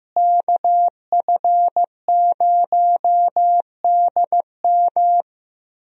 Morse code, heard as a keyed tone, about 15 wpm, 700 Hz. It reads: KF0DM
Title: USA Call Signs - 5 Characters - Repeated 2x - 15wpm